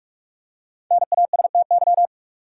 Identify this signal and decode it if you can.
Morse code, heard as a keyed tone, about 47 wpm, 700 Hz. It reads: NASTY
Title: Five Letter Words - 45wpm